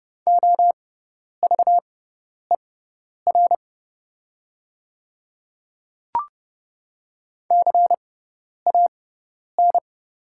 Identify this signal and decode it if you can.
Morse code, heard as a keyed tone, about 30 wpm, spaced as for 10 wpm, 690 Hz. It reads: OVER CAN
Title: 6x Character Spacing - Top 100 Words - 30wpm